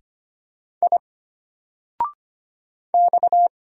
Morse code, heard as a keyed tone, about 25 wpm, 700 Hz. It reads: I X
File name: Single-Letter-Number-25wpm